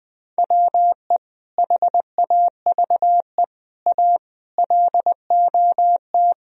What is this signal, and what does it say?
Morse code, heard as a keyed tone, about 20 wpm, 700 Hz. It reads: WE HAVE A LOT
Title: Speed-Racing - Sets of 5 Words 17wpm